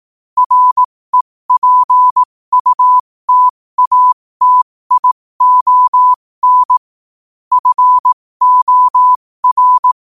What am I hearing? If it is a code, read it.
Morse code, heard as a keyed tone, about 18 wpm, spaced as for 15 wpm, 1000 Hz. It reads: REPUTATION FOR
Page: New quotes every day in morse code at 15 Words per minute.